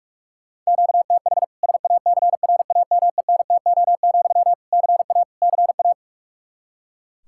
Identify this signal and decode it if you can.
Morse code, heard as a keyed tone, about 41 wpm, 700 Hz. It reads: QTH SACRAMENTO, CA CA
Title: K1USN SST QSO Practice 45wpm